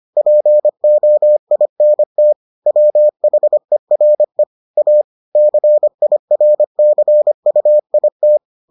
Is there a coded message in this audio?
Yes: Morse code from 0.2 to 8.4 s, heard as a keyed tone, about 25 wpm, 600 Hz. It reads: POINT WHERE A CIRCUIT